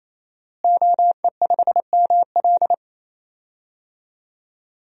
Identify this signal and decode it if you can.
Morse code, heard as a keyed tone, about 28 wpm, 700 Hz. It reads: OE5ML
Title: DXCC-100 Call Signs - 5 Characters - Repeated 3x - 28wpm